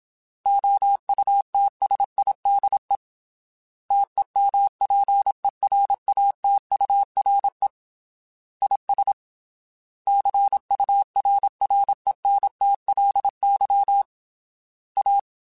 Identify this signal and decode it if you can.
Morse code, heard as a keyed tone, about 27 wpm, 800 Hz. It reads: OUTSIDE TEMPERATURE IS CURRENTLY A
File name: Weather Sets / Weather Sets-mp3s / Weather 5 5 @25